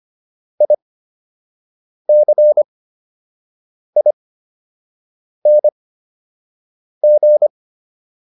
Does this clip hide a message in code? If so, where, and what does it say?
Yes: Morse code from 0.6 to 7.5 s, heard as a keyed tone, about 25 wpm, spaced as for 6 wpm, 600 Hz. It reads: ICING